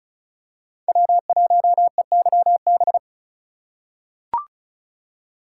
Morse code, heard as a keyed tone, about 36 wpm, 700 Hz. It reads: W1EYB
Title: USA Call Signs - 5 Characters - 4x Character Spacing - 35wpm